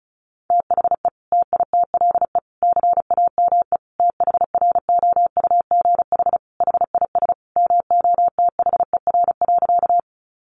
Morse code, heard as a keyed tone, about 34 wpm, 700 Hz. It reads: THE TITLE CAME THROUGH HIS MOTHER.
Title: Sentences from Top 1200 Words 35wpm